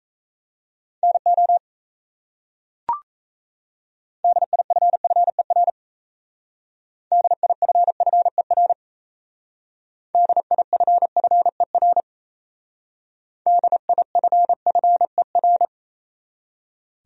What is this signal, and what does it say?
Morse code, heard as a keyed tone, about 34 wpm, 700 Hz. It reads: NO DIFFER DIFFER DIFFER DIFFER
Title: Speed-Racing Top 500 Words 28wpm